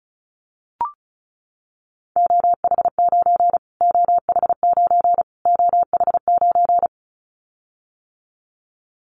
Morse code, heard as a keyed tone, about 34 wpm, 700 Hz. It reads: OH9 OH9 OH9
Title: Callsign Prefixes for Top 25 DXCC - Repeated 3x - 35wpm